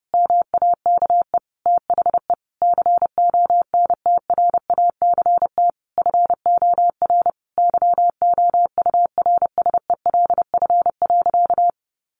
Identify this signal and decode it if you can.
Morse code, heard as a keyed tone, about 30 wpm, 700 Hz. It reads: MAKE THE CONTRACT FOR YOURSELF.
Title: Sentences from Top 1400 Words 30wpm